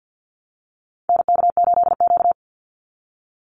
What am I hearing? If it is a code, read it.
Morse code, heard as a keyed tone, about 50 wpm, 700 Hz. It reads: NK8Q
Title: NA SOTA Activator Callsigns 50wpm